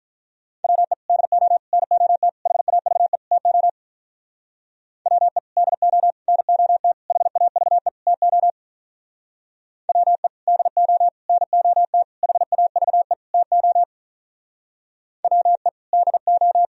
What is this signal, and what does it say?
Morse code, heard as a keyed tone, about 43 wpm, 700 Hz. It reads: WE DO NOT HAVE TO WE DO NOT HAVE TO WE DO NOT HAVE TO WE DO
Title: Speed-Racing - Sets of 5 Words 35wpm